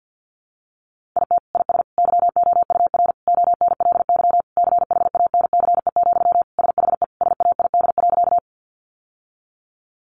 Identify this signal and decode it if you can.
Morse code, heard as a keyed tone, about 50 wpm, 700 Hz. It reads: IT IS YOUR ONLY CHANCE, SHE SAID.
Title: Sentences from Top 800 Words 50wpm